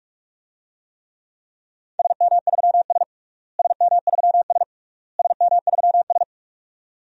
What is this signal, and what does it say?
Morse code, heard as a keyed tone, about 46 wpm, 700 Hz. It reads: SM3S SM3S SM3S
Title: DXCC-100 Call Signs - 4 Characters - Repeated 3x - 45wpm